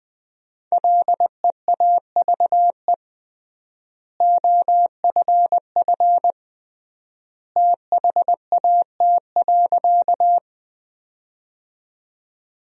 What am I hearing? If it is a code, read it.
Morse code, heard as a keyed tone, about 20 wpm, 700 Hz. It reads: LEAVE OFF THAT.
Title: 3x Word-Spacing - Sentences from Top 400 Words - 20wpm